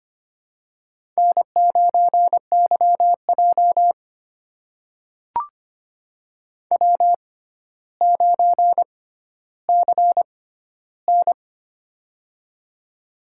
Morse code, heard as a keyed tone, about 25 wpm, spaced as for 13 wpm, 700 Hz. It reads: N9YJ W9CN
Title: USA Call Signs - 6x Character Spacing - 25wpm